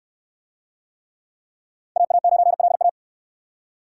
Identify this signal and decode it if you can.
Morse code, heard as a keyed tone, about 69 wpm, 700 Hz. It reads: AA9LA